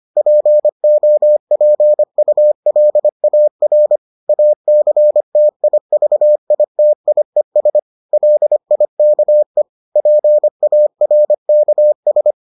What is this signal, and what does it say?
Morse code, heard as a keyed tone, about 25 wpm, 600 Hz. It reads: POPULAR ACTIVITIES LIKE PARKS